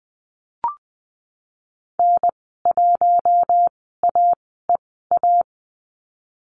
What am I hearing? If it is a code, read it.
Morse code, heard as a keyed tone, about 20 wpm, spaced as for 15 wpm, 700 Hz. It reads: N1AEA
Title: 088 Lesson 15g - Number 5 - Call Signs 20wpm